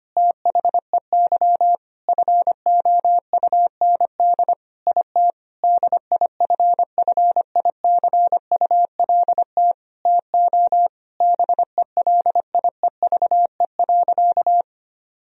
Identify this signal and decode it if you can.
Morse code, heard as a keyed tone, about 25 wpm, 700 Hz. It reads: THEY FOUND IT DIFFICULT TO BELIEVE.
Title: Sentences from Top 1000 Words - Review Part 4 - 25wpm